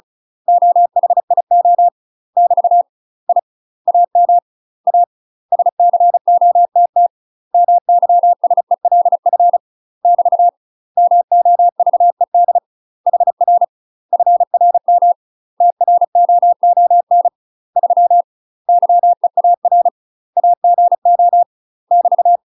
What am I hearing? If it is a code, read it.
Morse code, heard as a keyed tone, about 37 wpm, 700 Hz. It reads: OHIO <BT> I AM A SCOTT MYSELF <BT> MOVED HR FRM TROON 3 YEAR AGO <BT>
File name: UK MASTER QSOs TRAINING / 35WPM / UK MASTER QSOs 0015